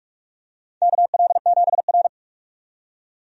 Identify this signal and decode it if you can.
Morse code, heard as a keyed tone, about 46 wpm, 700 Hz. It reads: KL7R
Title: USA Call Signs - 4 Characters - 45wpm